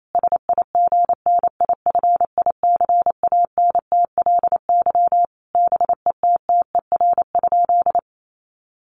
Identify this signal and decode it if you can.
Morse code, heard as a keyed tone, about 28 wpm, 700 Hz. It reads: SIGNIFICANTLY BETTER?
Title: Sentences from Top 1500 Words 28wpm